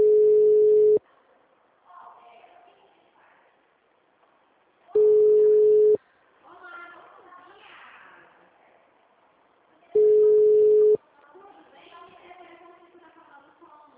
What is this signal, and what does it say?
Ringback tone (Europe, 425 Hz), with speech.
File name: out- -2013- - - .13702